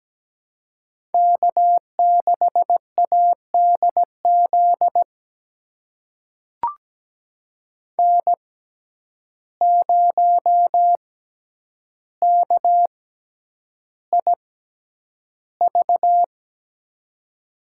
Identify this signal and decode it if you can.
Morse code, heard as a keyed tone, about 17 wpm, spaced as for 9 wpm, 700 Hz. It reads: K6ADZ N0KIV
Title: USA Call Signs - 5 Characters - 6x Character Spacing - 17wpm